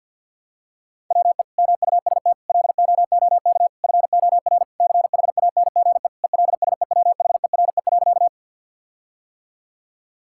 Morse code, heard as a keyed tone, about 50 wpm, 700 Hz. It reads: WE MUST LOOK FOR CHANGE ELSEWHERE.
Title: Sentences from Top 1600 Words 50wpm